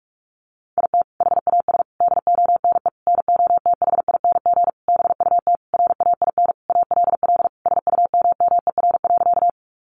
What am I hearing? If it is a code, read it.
Morse code, heard as a keyed tone, about 44 wpm, 700 Hz. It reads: IT HAS DONE NOTHING BUT RAIN ALL SUMMER.
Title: Sentences from Top 1100 Words 45wpm